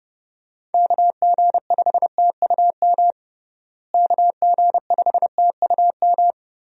Morse code, heard as a keyed tone, about 30 wpm, 700 Hz. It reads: KG5TUM KG5TUM
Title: USA Call Signs - 6 Characters - Repeated 3x - 30wpm